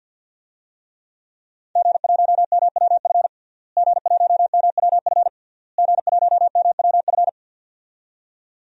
Morse code, heard as a keyed tone, about 52 wpm, 700 Hz. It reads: G1MWF G1MWF G1MWF
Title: DXCC-100 Call Signs - 5 Characters - Repeated 3x - 50wpm